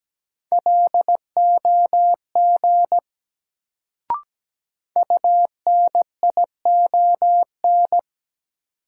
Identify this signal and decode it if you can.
Morse code, heard as a keyed tone, about 17 wpm, 700 Hz. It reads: LOG UNION
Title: Sentences from Top 1400 Words - New Word Drill Alt - 17wpm